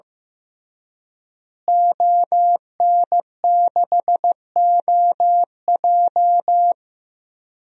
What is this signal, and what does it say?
Morse code, heard as a keyed tone, about 15 wpm, 700 Hz. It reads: ON6OJ
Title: DXCC-100 Call Signs - 5 Characters - Repeated 3x - 15wpm